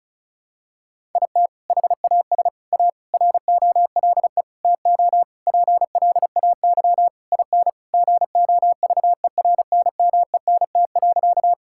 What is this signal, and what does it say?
Morse code, heard as a keyed tone, about 36 wpm, 700 Hz. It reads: IT HAS A ROLE TO PLAY IN GOVERNMENT.
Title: Sentences from Top 1000 Words - Review Part 6 - 35wpm